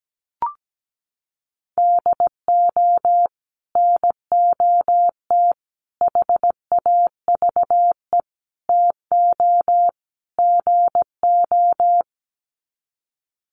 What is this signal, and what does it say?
Morse code, heard as a keyed tone, about 17 wpm, 700 Hz. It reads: DO NOT HAVE TO GO
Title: Sets of 5 Words 17wpm